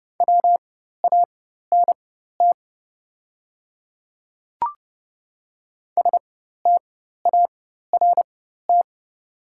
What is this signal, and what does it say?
Morse code, heard as a keyed tone, about 30 wpm, spaced as for 14 wpm, 700 Hz. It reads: WANT START
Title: 4x Character Spacing - Top 300 Words - 30wpm